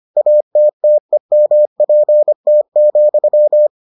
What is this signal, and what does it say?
Morse code, heard as a keyed tone, about 25 wpm, 600 Hz. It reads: ATTEMPT,